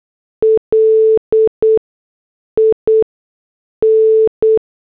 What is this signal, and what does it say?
Morse code, heard as a keyed tone, about 8 wpm, 440 Hz. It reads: L I N
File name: morse / headlines